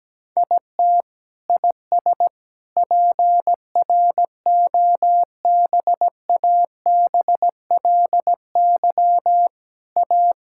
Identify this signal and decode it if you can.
Morse code, heard as a keyed tone, about 17 wpm, 700 Hz. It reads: IT IS PROBABLY A
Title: Sentences from Top 1000 Words - Review Part 6 - 17wpm